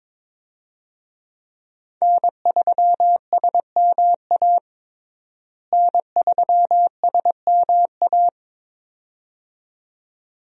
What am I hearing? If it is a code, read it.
Morse code, heard as a keyed tone, about 22 wpm, 700 Hz. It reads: N3SMA N3SMA
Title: USA Call Signs - 5 Characters - Repeated 2x - 22wpm